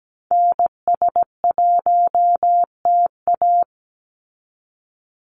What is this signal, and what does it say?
Morse code, heard as a keyed tone, about 17 wpm, 700 Hz. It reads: NS1TA
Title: NA SOTA Activator Callsigns 17wpm